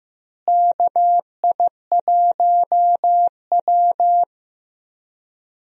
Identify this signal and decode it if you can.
Morse code, heard as a keyed tone, about 15 wpm, 700 Hz. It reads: KI1W
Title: USA Call Signs - 4 Characters - Repeated 3x - 15wpm